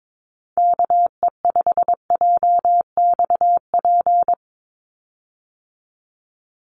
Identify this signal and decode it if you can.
Morse code, heard as a keyed tone, about 22 wpm, 700 Hz. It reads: KE5JXP
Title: USA Call Signs 22wpm